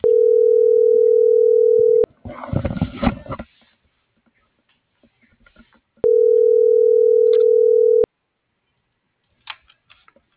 Ringback tone (North America), 440 + 480 Hz.